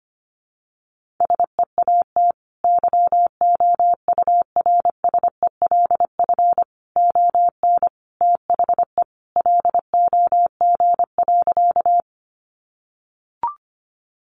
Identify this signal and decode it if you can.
Morse code, heard as a keyed tone, about 25 wpm, 700 Hz. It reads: SEAT YOURSELF ON THE LOG.
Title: Sentences from Top 1500 Words 25wpm